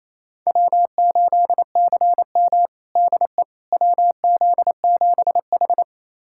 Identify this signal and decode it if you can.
Morse code, heard as a keyed tone, about 27 wpm, 700 Hz. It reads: W8CM DE WZ7H
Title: USA Callsign DE Callsign - 4 Characters - 28wpm